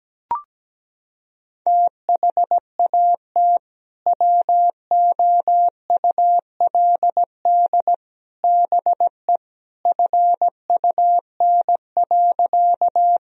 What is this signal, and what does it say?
Morse code, heard as a keyed tone, about 17 wpm, 700 Hz. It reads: THAT WOULD BE FUN.
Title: Sentences from Top 1000 Words - Review Part 4 - 17wpm